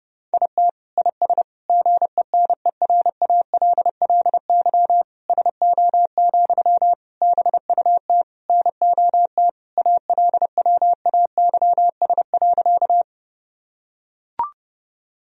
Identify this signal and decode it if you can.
Morse code, heard as a keyed tone, about 32 wpm, 700 Hz. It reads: IT IS GENERALLY SO, BUT NOT ALWAYS.
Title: Sentences from Top 1000 Words - Review Part 4 - 30wpm